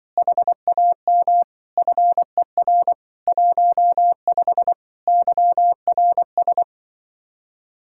Morse code, heard as a keyed tone, about 24 wpm, 700 Hz. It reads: HAM FER 15 YRS
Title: Speed-Racing Common QSO Phrases Encore - 20wpm